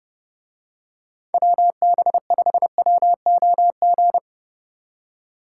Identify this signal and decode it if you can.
Morse code, heard as a keyed tone, about 30 wpm, 700 Hz. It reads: WB5WOG